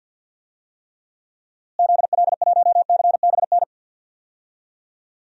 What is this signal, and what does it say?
Morse code, heard as a keyed tone, about 52 wpm, 700 Hz. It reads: ZL1CBN
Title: DXCC-100 Call Signs - 6 Characters - Repeated 3x - 50wpm